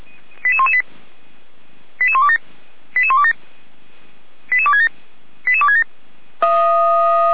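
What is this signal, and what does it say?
Five-tone selective call (ZVEI), heard as quick runs of five tones, tones of about 70 ms. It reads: 88118 88127 88127 88267 88267
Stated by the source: ZVEI five-tone call 88199 received by SDR, archived by feuerwehr-orlamuende.org fire brigade.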